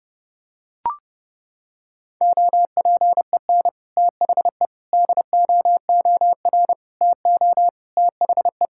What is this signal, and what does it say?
Morse code, heard as a keyed tone, about 30 wpm, 700 Hz. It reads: OPEN THE DOOR TO THE
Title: Sentences from Top 1000 Words - Review Part 6 - 30wpm